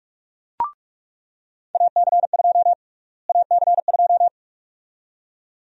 Morse code, heard as a keyed tone, about 46 wpm, 700 Hz. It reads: AC2 AC2
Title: USA Call Sign Prefixes - Repeated 2x - 45wpm